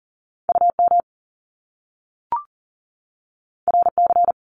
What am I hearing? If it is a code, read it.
Morse code, heard as a keyed tone, about 40 wpm, 700 Hz. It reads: UM RC
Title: Top Two Letter Combinations in Words 40wpm